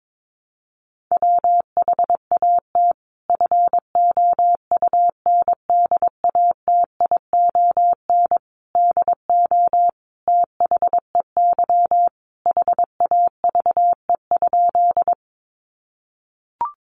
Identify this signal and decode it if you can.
Morse code, heard as a keyed tone, about 22 wpm, 700 Hz. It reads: WHAT FOUNDATION DO THEY HAVE?
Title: Sentences from Top 1200 Words 22wpm